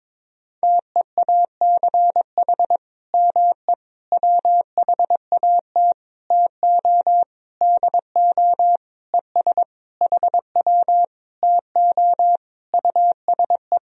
Morse code, heard as a keyed tone, about 22 wpm, 700 Hz. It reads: TEACH ME WHAT TO DO ES HW TO USE
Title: Sentences from Top 500 Words and Abbreviations 22wpm